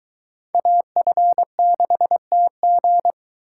Morse code, heard as a keyed tone, about 22 wpm, 700 Hz. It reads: AF6TG